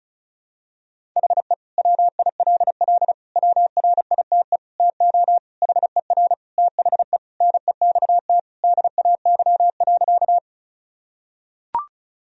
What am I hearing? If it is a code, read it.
Morse code, heard as a keyed tone, about 36 wpm, 700 Hz. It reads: HE WILL WRITE TO HER THE NEXT DAY.
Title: Sentences from Top 1000 Words - Review Part 1 - 35wpm